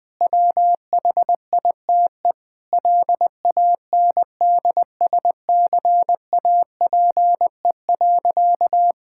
Morse code, heard as a keyed tone, about 20 wpm, 700 Hz. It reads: WHITE LANDSCAPE.
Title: Difficult Sentences with Hints - 20wpm